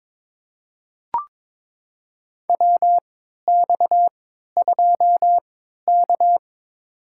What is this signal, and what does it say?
Morse code, heard as a keyed tone, about 22 wpm, spaced as for 13 wpm, 700 Hz. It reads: WX2K
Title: USA Call Signs - 4 Characters - 3x Character Spacing - 22wpm